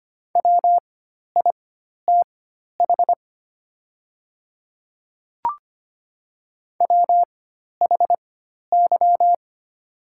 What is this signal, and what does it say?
Morse code, heard as a keyed tone, about 25 wpm, spaced as for 12 wpm, 700 Hz. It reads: WITH WHY
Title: 4x Character Spacing - Top 200 Words - 25wpm